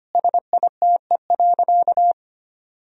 Morse code, heard as a keyed tone, about 25 wpm, 700 Hz. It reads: SITE.
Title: Sentences from Top 1000 Words - Review Part 4 - 25wpm